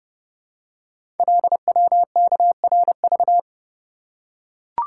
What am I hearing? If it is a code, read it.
Morse code, heard as a keyed tone, about 29 wpm, 700 Hz. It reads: LWKRV
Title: Code Groups - 5 Letters - 30wpm